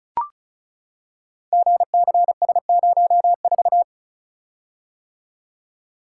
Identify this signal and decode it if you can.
Morse code, heard as a keyed tone, about 35 wpm, 700 Hz. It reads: GCS04
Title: Code Groups - 5 Letter:Numbers - 35wpm